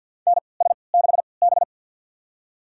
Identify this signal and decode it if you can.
Morse code, heard as a keyed tone, about 53 wpm, spaced as for 29 wpm, 700 Hz. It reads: NS6B
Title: USA Call Signs - 4 Characters - 3x Character Spacing - 50wpm